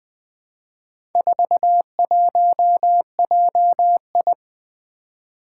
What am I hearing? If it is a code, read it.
Morse code, heard as a keyed tone, about 20 wpm, 700 Hz. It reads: 41JI
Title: Four Challenging Letter-Number Combinations 20wpm